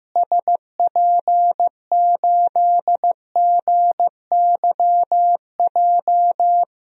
Morse code, heard as a keyed tone, about 15 wpm, 700 Hz. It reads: SP8GYJ
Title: DXCC-100 Call Signs - 6 Characters - Repeated 2x - 15wpm